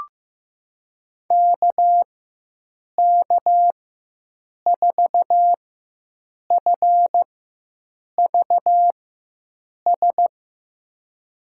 Morse code, heard as a keyed tone, about 15 wpm, spaced as for 7 wpm, 700 Hz. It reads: KK4FVS
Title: USA Call Signs - 6 Characters - 4x Character Spacing - 15wpm